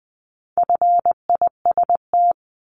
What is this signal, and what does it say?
Morse code, heard as a keyed tone, about 20 wpm, 700 Hz. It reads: FIST
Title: 260 Lesson 42e - Words 20-20wpm